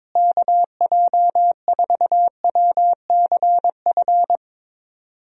Morse code, heard as a keyed tone, about 22 wpm, 700 Hz. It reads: KJ4WCF
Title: USA Call Signs - 6 Characters - Repeated 3x - 22wpm